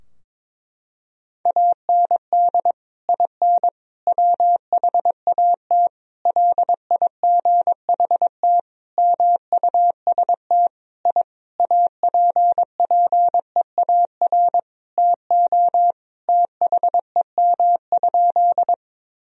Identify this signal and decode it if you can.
Morse code, heard as a keyed tone, about 22 wpm, 700 Hz. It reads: AND IN WHAT LIGHT MUST I APPEAR TO THEM?
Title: Sentences from Top 1000 Words - Review Part 2 - 22wpm